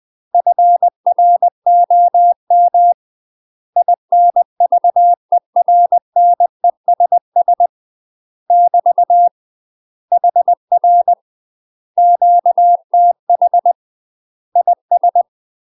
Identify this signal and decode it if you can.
Morse code, heard as a keyed tone, about 20 wpm, 700 Hz. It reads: FROM INVERNESS <BT> HR QTH IS
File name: UK MASTER QSOs TRAINING / 20WPM / UK MASTER QSOs 0055